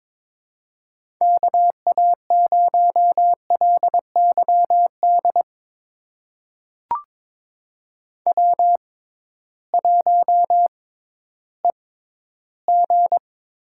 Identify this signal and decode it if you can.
Morse code, heard as a keyed tone, about 22 wpm, spaced as for 13 wpm, 700 Hz. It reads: KA0LYD W1EG
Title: USA Call Signs - 6x Character Spacing - 22wpm